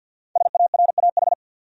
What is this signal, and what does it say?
Morse code, heard as a keyed tone, about 48 wpm, 700 Hz. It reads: SARAH
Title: USA Common Names - 50wpm